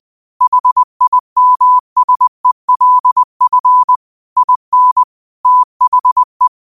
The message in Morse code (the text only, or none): HIMSELF IN THE